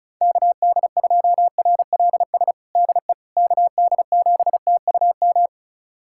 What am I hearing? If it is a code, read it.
Morse code, heard as a keyed tone, about 34 wpm, 690 Hz. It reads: KD2RLS DE KD7TUM
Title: USA Callsign DE Callsign - 6 Characters - 35wpm